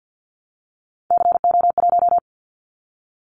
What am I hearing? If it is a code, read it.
Morse code, heard as a keyed tone, about 50 wpm, 700 Hz. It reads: CO1
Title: Callsign Prefixes for Top 100 DXCC - 50wpm